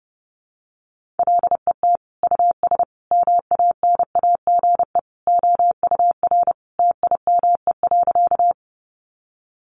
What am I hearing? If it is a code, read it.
Morse code, heard as a keyed tone, about 30 wpm, 700 Hz. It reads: LET US MANAGE OUR TIME.
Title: Sentences from Top 1400 Words 30wpm